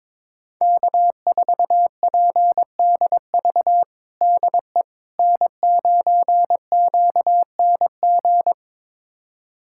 Morse code, heard as a keyed tone, about 22 wpm, 700 Hz. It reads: K4PDV DE N9QNG
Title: USA Callsign DE Callsign 22wpm